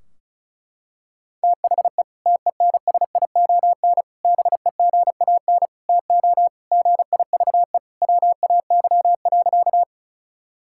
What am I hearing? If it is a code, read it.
Morse code, heard as a keyed tone, about 34 wpm, 700 Hz. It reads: THE TENSION BEGAN TO GIVE WAY.
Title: Sentences from Top 1600 Words 35wpm